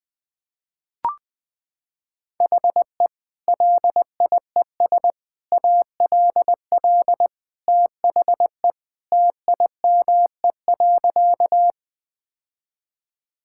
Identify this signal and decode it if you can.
Morse code, heard as a keyed tone, about 20 wpm, 700 Hz. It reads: HE LIES ALL THE TIME.
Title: Sentences with 3-Word Phrases - 20wpm